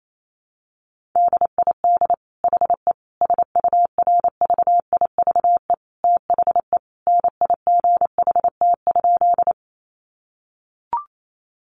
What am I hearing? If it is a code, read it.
Morse code, heard as a keyed tone, about 28 wpm, 700 Hz. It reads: DID HE SURVIVE THE NIGHT?
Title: Sentences from Top 1300 Words 28wpm